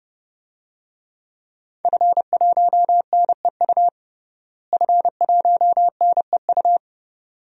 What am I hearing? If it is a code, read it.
Morse code, heard as a keyed tone, about 32 wpm, 700 Hz. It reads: F1NEU F1NEU
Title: DXCC-100 Call Signs - 5 Characters - Repeated 2x - 30wpm